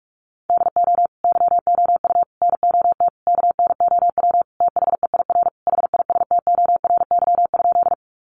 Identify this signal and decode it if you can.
Morse code, heard as a keyed tone, about 44 wpm, 700 Hz. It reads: DO YOU NOT KNOW THEIR HISTORY?
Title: Sentences from Top 600 Words 45wpm